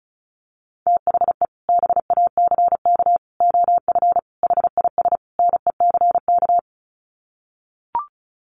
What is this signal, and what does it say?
Morse code, heard as a keyed tone, about 34 wpm, 700 Hz. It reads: THE BACK OF HIS NECK
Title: Sets of 5 Words Encore 35wpm